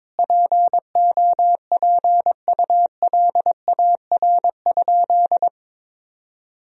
Morse code, heard as a keyed tone, about 22 wpm, 700 Hz. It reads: POPULAR?
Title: Sentences from Top 1000 Words - Review Part 3 - 22wpm